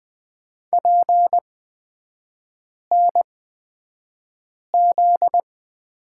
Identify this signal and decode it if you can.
Morse code, heard as a keyed tone, about 20 wpm, spaced as for 5 wpm, 700 Hz. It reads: PNZ
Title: 243 Lesson 39e - Letter z - Three Characters 20wpm